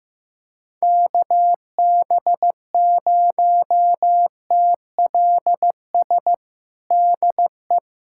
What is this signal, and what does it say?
Morse code, heard as a keyed tone, about 15 wpm, 700 Hz. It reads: KB0TLS DE
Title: USA Callsign DE Callsign - 6 Characters - 15wpm